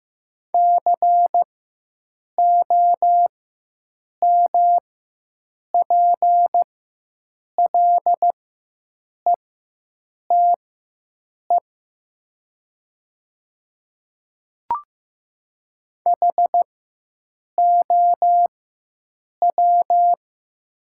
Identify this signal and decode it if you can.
Morse code, heard as a keyed tone, about 15 wpm, spaced as for 7 wpm, 700 Hz. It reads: COMPLETE HOW
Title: 4x Character Spacing - Top 500 Words - 15wpm